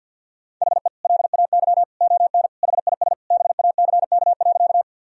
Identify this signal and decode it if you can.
Morse code, heard as a keyed tone, about 53 wpm, 700 Hz. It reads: HE LAY ON HIS BACK.
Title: Sentences from Top 1000 Words - Review Part 2 - 50wpm